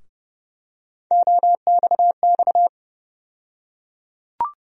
Morse code, heard as a keyed tone, about 30 wpm, 700 Hz. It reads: OXX
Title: USA Call Sign Postfixes - 3 Letters 30wpm